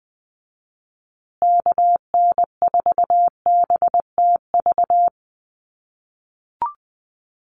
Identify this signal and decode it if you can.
Morse code, heard as a keyed tone, about 20 wpm, 700 Hz. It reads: KN4BTV
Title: USA Call Signs Repeated 2x 20wpm